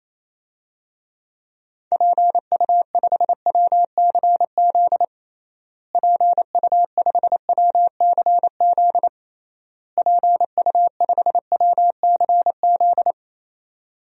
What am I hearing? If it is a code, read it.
Morse code, heard as a keyed tone, about 28 wpm, 700 Hz. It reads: PU5WCZ PU5WCZ PU5WCZ
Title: DXCC-100 Call Signs - 6 Characters - Repeated 3x - 28wpm